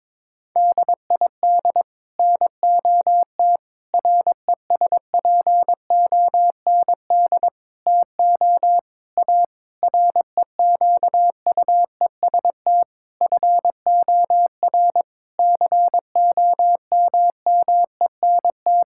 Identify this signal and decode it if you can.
Morse code, heard as a keyed tone, about 22 wpm, 700 Hz. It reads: DID NOT RESPOND TO A REQUEST FOR COMMENT
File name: practice / 22wpm / Sets of 8 Words - 22wpm